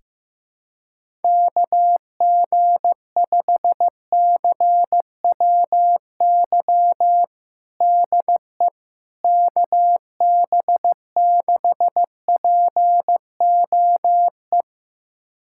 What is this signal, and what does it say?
Morse code, heard as a keyed tone, about 15 wpm, 700 Hz. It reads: KG5CWY DE KB6POE
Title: USA Callsign DE Callsign - 6 Characters - 15wpm